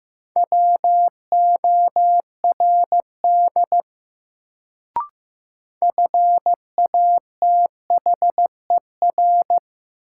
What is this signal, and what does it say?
Morse code, heard as a keyed tone, about 15 wpm, 700 Hz. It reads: WORD FATHER
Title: Sentences from Top 300 Words - New Word Drill - 15wpm